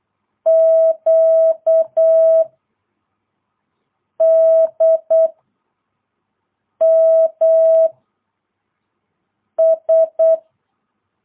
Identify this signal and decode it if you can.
Morse code, heard as a keyed tone, about 8 wpm, 650 Hz. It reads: Q D M S
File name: jh8cbh / 2022cw